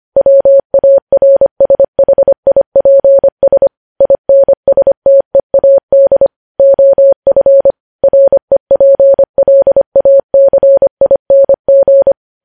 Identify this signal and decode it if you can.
Morse code, heard as a keyed tone, about 25 wpm, 550 Hz. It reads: WARSHIPS INSTEAD OF REPLACING
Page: news-headlines-in-morse-code-at-25-wpm